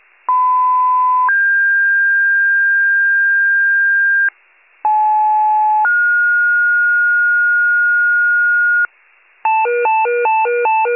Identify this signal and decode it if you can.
Two-tone page, heard as a one-second tone, then a three-second tone, about 1010 then 1600 Hz, about 860 then 1430 Hz, with noise. It is heard from 0.3 to 8.9 s.